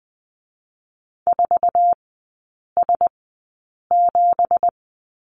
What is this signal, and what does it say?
Morse code, heard as a keyed tone, about 20 wpm, spaced as for 8 wpm, 690 Hz. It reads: 4S7
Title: 166 Lesson 27e - Number 7 - Three Characters 20wpm